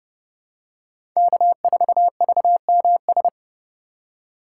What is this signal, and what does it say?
Morse code, heard as a keyed tone, about 30 wpm, 700 Hz. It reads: K4VMS
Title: USA Call Signs - 5 Characters - 6x Character Spacing - 30wpm